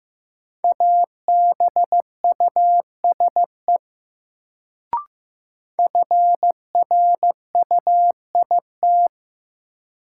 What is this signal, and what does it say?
Morse code, heard as a keyed tone, about 15 wpm, 700 Hz. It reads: ABUSE FRUIT
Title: Sentences from Top 1300 Words - New Word Drill - 15wpm